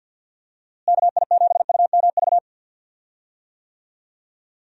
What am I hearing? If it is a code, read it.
Morse code, heard as a keyed tone, about 50 wpm, 700 Hz. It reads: KI7UMV
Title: USA Call Signs - 6 Characters - Repeated 3x - 50wpm